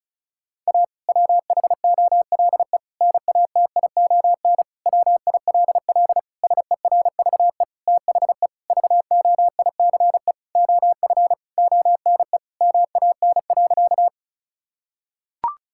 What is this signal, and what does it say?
Morse code, heard as a keyed tone, about 36 wpm, 700 Hz. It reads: A WHOLE NATION WILL SERVE THE VOICE OF ONE MAN.
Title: Sentences from Top 1000 Words - Review Part 6 - 35wpm